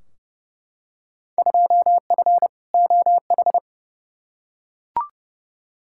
Morse code, heard as a keyed tone, about 32 wpm, 700 Hz. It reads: 2F OH
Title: Field Day Exchanges - 30wpm